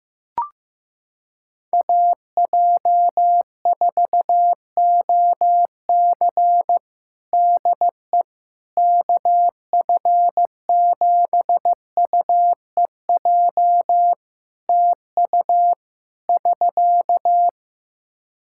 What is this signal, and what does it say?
Morse code, heard as a keyed tone, about 15 wpm, 700 Hz. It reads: AJ4OC DE KF7UEJ TU <SK>
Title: Practice QSO - Exchange 4 - Ending - 15wpm